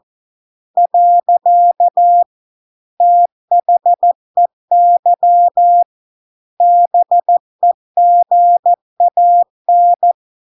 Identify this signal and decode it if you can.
Morse code, heard as a keyed tone, about 14 wpm, 700 Hz. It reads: . THEY BEGAN